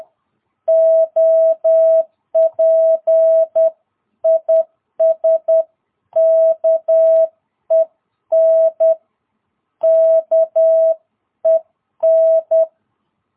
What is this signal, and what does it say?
Morse code, heard as a keyed tone, about 10 wpm, 660 Hz. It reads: OP IS KEN KEN
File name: jh8cbh / 2022cw